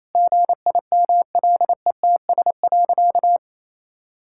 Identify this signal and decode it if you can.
Morse code, heard as a keyed tone, about 28 wpm, 700 Hz. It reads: GIMLETS.
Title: Pangram Challenge v2 - 28wpm